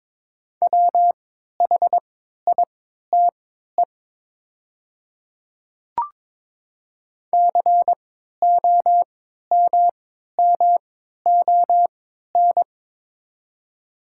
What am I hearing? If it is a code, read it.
Morse code, heard as a keyed tone, about 22 wpm, spaced as for 12 wpm, 700 Hz. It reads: WHITE COMMON
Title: 3x Character Spacing - Top 500 Words - 22wpm